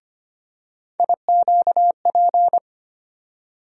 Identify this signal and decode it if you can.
Morse code, heard as a keyed tone, about 25 wpm, 700 Hz. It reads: IQP
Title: USA Call Sign Postfixes - 3 Letters 25wpm